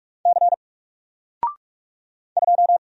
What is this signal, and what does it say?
Morse code, heard as a keyed tone, about 44 wpm, 700 Hz. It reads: C 2
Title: Single Character (Letter:Number:Punctuation:Prosign) 45wpm